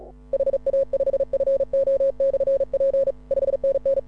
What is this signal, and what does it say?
Morse code, heard as a keyed tone, about 36 wpm, 540 Hz. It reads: HA5FO/P 5NN
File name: AUDIO2018 / HA6OI CW